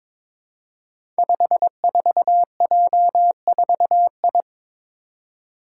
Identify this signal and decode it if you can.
Morse code, heard as a keyed tone, about 22 wpm, 700 Hz. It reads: 54J4I
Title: Five Challenging Letter-Number Combinations 22wpm